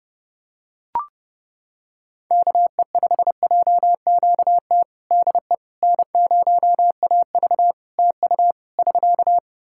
Morse code, heard as a keyed tone, about 30 wpm, 700 Hz. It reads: KE5JQT DE N0AV TU <SK>
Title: Practice QSO - Exchange 4 - Ending - 30wpm